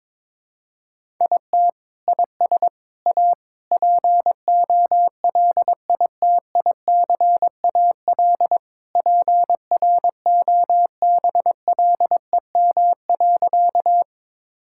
Morse code, heard as a keyed tone, about 22 wpm, 700 Hz. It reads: IT IS A POLITICAL PROBLEM.
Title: Sentences from Top 1000 Words - Review Part 5 - 22wpm